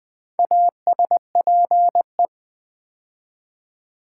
Morse code, heard as a keyed tone, about 20 wpm, 700 Hz. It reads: ASPE
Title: Build A Word - 2 Letters At A Time - 6-Letter Words - 20wpm